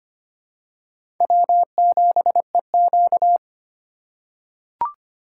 Morse code, heard as a keyed tone, about 25 wpm, 700 Hz. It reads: W7EQ
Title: USA Call Signs - 2x Character Spacing - 25wpm